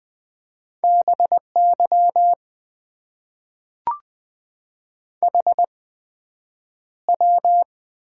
Morse code, heard as a keyed tone, about 21 wpm, spaced as for 9 wpm, 700 Hz. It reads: BY HW
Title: Top 90 QSO Elements - 8x Character Spacing - 20wpm